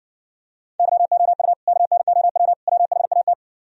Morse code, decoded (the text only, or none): YOU KNOW WHAT